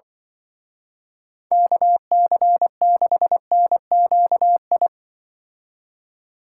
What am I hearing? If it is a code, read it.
Morse code, heard as a keyed tone, about 23 wpm, 700 Hz. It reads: KC6NQI
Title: Speed Racing - USA 2x3 Call Signs 20wpm